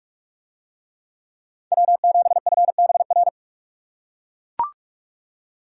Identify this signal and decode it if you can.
Morse code, heard as a keyed tone, about 41 wpm, 700 Hz. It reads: W7FBR